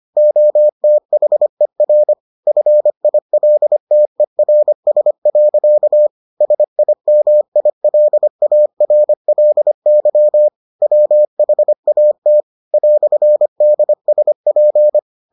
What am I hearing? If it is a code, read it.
Morse code, heard as a keyed tone, about 25 wpm, 600 Hz. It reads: OTHER FILTERS. SIMILARLY WHAT "DSP